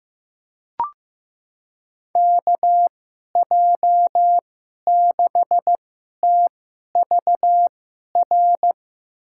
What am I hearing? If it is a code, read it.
Morse code, heard as a keyed tone, about 15 wpm, spaced as for 11 wpm, 700 Hz. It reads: KJ6TVR
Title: USA Call Signs - 6 Characters - 2x Character Spacing - 15wpm